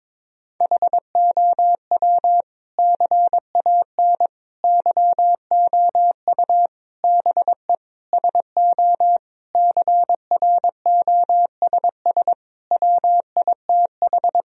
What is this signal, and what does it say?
Morse code, heard as a keyed tone, about 22 wpm, 700 Hz. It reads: HOW CAN YOU BE SO CROSS WITH
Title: Sentences from Top 1000 Words - Review Part 1 - 22wpm